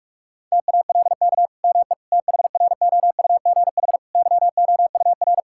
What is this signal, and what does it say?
Morse code, heard as a keyed tone, about 46 wpm, 700 Hz. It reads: TALK ME THROUGH YOUR
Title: Sentences from Top 1000 Words - Review Part 5 - 45wpm